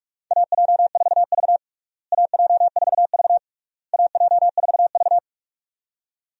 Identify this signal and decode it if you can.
Morse code, heard as a keyed tone, about 42 wpm, 700 Hz. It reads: AJ4V AJ4V AJ4V
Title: USA Call Signs - 4 Characters - Repeated 3x - 45wpm